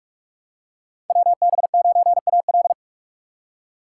Morse code, heard as a keyed tone, about 45 wpm, 700 Hz. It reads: WB9AL
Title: USA Call Signs - 6x Character Spacing - 45wpm